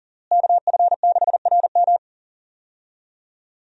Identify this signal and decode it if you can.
Morse code, heard as a keyed tone, about 40 wpm, 700 Hz. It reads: KF6RM
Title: USA Call Signs - 5 Characters - Repeated 3x - 40wpm